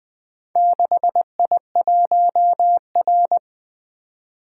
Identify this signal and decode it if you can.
Morse code, heard as a keyed tone, about 21 wpm, 700 Hz. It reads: 6I1R
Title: Four Letter-Number Combinations - 20wpm